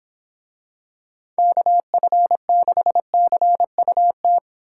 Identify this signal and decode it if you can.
Morse code, heard as a keyed tone, about 26 wpm, 700 Hz. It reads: KF6CUT
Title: Speed Racing - USA 2x3 Call Signs 22wpm